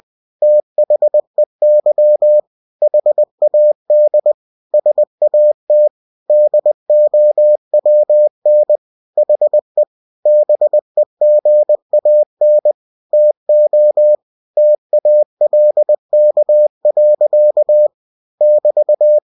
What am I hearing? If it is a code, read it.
Morse code, heard as a keyed tone, about 20 wpm, 600 Hz. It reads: THEY HAD SAT DOWN HE BEGAN TO TALK. <BT>
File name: cw-books / winnie-the-pooh / winnie-20wpm-page-0106